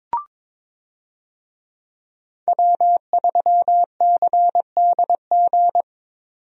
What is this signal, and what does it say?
Morse code, heard as a keyed tone, about 22 wpm, 700 Hz. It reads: W3CDG
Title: USA Call Signs - 5 Characters - 22wpm